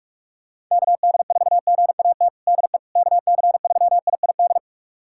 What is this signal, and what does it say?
Morse code, heard as a keyed tone, about 46 wpm, 700 Hz. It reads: KD4GAT DE KC3IID
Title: USA Callsign DE Callsign - 6 Characters - 45wpm